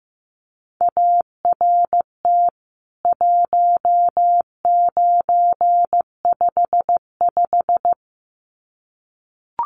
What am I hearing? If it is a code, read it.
Morse code, heard as a keyed tone, about 15 wpm, 700 Hz. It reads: ART 1955
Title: CWT Exchange 15wpm